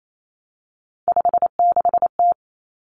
Morse code, heard as a keyed tone, about 28 wpm, 700 Hz. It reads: 56T
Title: Three Challenging Letter-Number Combinations 28wpm